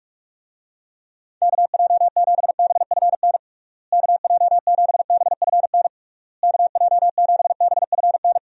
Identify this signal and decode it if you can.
Morse code, heard as a keyed tone, about 42 wpm, 700 Hz. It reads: KJ7BFN KJ7BFN KJ7BFN
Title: USA Call Signs - 6 Characters - Repeated 3x - 45wpm